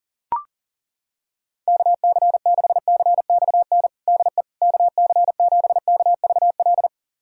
Morse code, heard as a keyed tone, about 37 wpm, 700 Hz. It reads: KC6CXN DE KC7KVL
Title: USA Callsign DE Callsign - 6 Characters - 40wpm